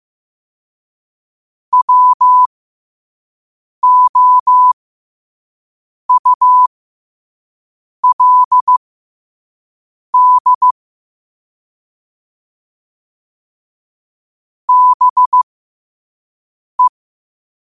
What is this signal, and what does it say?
Morse code, heard as a keyed tone, about 15 wpm, spaced as for 5 wpm, 1000 Hz. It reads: WOULD BE
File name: qotdfiles / 5wpm